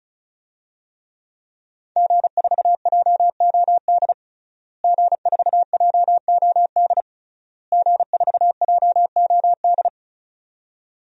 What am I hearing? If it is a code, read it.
Morse code, heard as a keyed tone, about 36 wpm, 700 Hz. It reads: G4JOD G4JOD G4JOD
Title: DXCC-100 Call Signs - 5 Characters - Repeated 3x - 35wpm